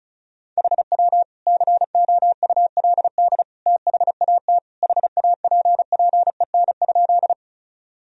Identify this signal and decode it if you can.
Morse code, heard as a keyed tone, about 34 wpm, 700 Hz. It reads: HW COULD THAT HAPPEN?
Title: Sentences from Top 400 Words and Abbreviations 35wpm